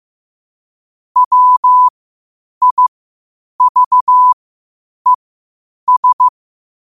Morse code, heard as a keyed tone, about 15 wpm, spaced as for 8 wpm, 1000 Hz. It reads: WIVES
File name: qotdfiles / 8wpm